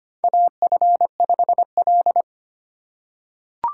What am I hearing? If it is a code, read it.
Morse code, heard as a keyed tone, about 25 wpm, 700 Hz. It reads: AF5L
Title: USA Call Signs - 8x Character Spacing - 25wpm